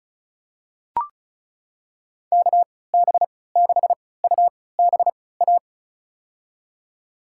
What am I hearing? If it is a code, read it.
Morse code, heard as a keyed tone, about 36 wpm, spaced as for 20 wpm, 700 Hz. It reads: KB6UBA
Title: USA Call Signs - 6 Characters - 3x Character Spacing - 35wpm